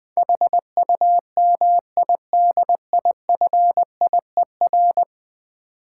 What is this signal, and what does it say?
Morse code, heard as a keyed tone, about 23 wpm, 700 Hz. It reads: HUMIDIFIER
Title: Difficult Words - Speed-Racing with Hints - 20wpm